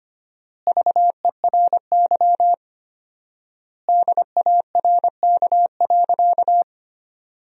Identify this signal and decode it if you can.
Morse code, heard as a keyed tone, about 25 wpm, 700 Hz. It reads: VERY DARK.
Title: 4x Word-Spacing - Sentences from Top 500 Words - 25wpm